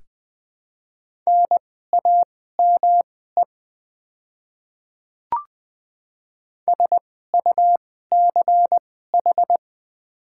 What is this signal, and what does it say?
Morse code, heard as a keyed tone, about 20 wpm, spaced as for 14 wpm, 700 Hz. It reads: NAME SUCH
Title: 2x Character Spacing - Top 300 Words - 20wpm